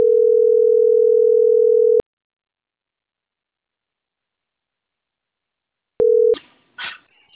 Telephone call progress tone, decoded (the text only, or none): ringback tone